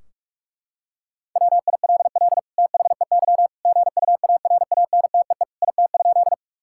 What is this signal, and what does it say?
Morse code, heard as a keyed tone, about 41 wpm, 700 Hz. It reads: WILL THEY GUARANTEE IT?
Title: Sentences from Top 1600 Words 45wpm